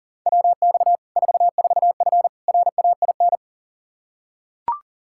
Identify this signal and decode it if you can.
Morse code, heard as a keyed tone, about 40 wpm, 700 Hz. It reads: WX 44F RAIN
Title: Practice QSO - Exchange 2 - Info - 40wpm